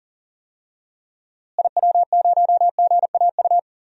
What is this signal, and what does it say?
Morse code, heard as a keyed tone, about 40 wpm, 700 Hz. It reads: IW0GAU
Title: DXCC-100 Call Signs - 6 Characters - Repeated 3x - 40wpm